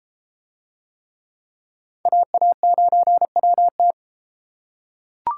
Morse code, heard as a keyed tone, about 33 wpm, 700 Hz. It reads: AA9WT